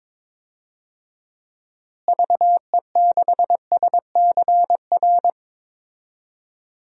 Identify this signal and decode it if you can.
Morse code, heard as a keyed tone, about 22 wpm, 700 Hz. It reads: VE6SCR
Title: DXCC-100 Call Signs - 6 Characters - Repeated 3x - 22wpm